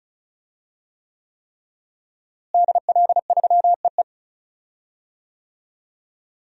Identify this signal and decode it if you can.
Morse code, heard as a keyed tone, about 35 wpm, 700 Hz. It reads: DL3EE
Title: DXCC-100 Call Signs - 5 Characters - Repeated 3x - 35wpm